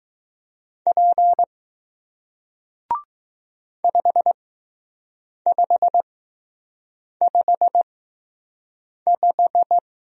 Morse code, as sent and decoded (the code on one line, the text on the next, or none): .--. / ..... ..... ..... .....
P 5555